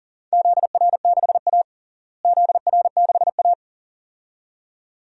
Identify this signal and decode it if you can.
Morse code, heard as a keyed tone, about 40 wpm, 700 Hz. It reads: ZR6A ZR6A
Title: DXCC-100 Call Signs - 4 Characters - Repeated 2x - 40wpm